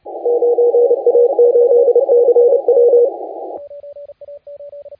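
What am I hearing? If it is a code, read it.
Morse code, heard as a keyed tone, about 29 wpm, 480 Hz, with radio noise. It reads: 9A8CW
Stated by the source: Morse contest audio recorded by LA4C during the SAC CW 2018 contest.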